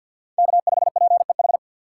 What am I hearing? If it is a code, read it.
Morse code, heard as a keyed tone, about 52 wpm, 700 Hz. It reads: K5PEH
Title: USA Call Signs - 5 Characters - 4x Character Spacing - 50wpm